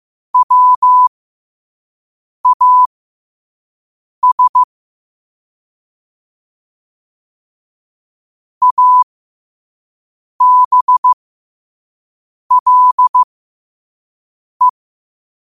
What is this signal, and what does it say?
Morse code, heard as a keyed tone, about 15 wpm, spaced as for 5 wpm, 1000 Hz. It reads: WAS ABLE
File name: qotdfiles / 5wpm